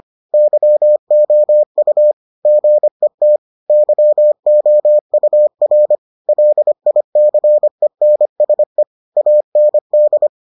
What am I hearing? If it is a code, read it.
Morse code, heard as a keyed tone, about 25 wpm, 600 Hz. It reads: YOU GET YOUR LICENSE AND